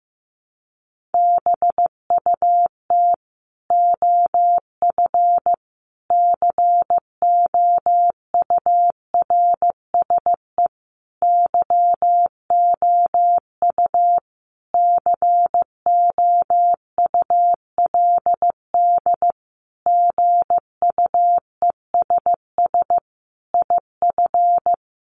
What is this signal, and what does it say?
Morse code, heard as a keyed tone, about 14 wpm, 700 Hz. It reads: BUT OF COURSE YOU COULD GUESS IF
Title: Sentences from Top 1500 Words 15wpm